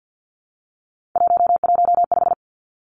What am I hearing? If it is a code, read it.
Morse code, heard as a keyed tone, about 50 wpm, 700 Hz. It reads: 115
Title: Three Challenging Letter-Number Combinations 50wpm